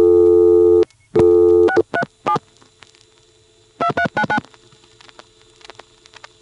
DTMF, heard as telephone touch tones, tones of about 70 ms. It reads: 637 2399